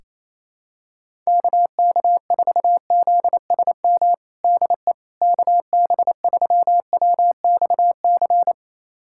Morse code, heard as a keyed tone, about 28 wpm, 700 Hz. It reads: KK4ZSM DE KB3WXC
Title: USA Callsign DE Callsign - 6 Characters - 28wpm